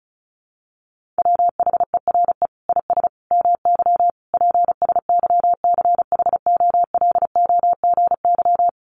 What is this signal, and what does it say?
Morse code, heard as a keyed tone, about 34 wpm, 700 Hz. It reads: WHERE IS MY PSYCHOLOGY
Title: Sentences from Top 1400 Words 35wpm